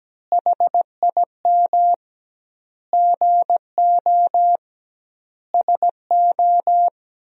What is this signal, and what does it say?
Morse code, heard as a keyed tone, about 17 wpm, 700 Hz. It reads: HIM GO SO
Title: 2x Word-Spacing - Sentences from Top 300 Words - 17wpm